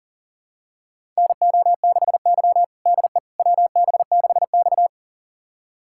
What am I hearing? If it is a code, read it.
Morse code, heard as a keyed tone, about 40 wpm, 700 Hz. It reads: NO6Y DE WB6X
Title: USA Callsign DE Callsign - 4 Characters - 40wpm